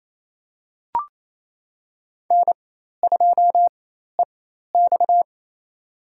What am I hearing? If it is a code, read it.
Morse code, heard as a keyed tone, about 27 wpm, spaced as for 13 wpm, 700 Hz. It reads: N2EX
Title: USA Call Signs - 4 Characters - 4x Character Spacing - 28wpm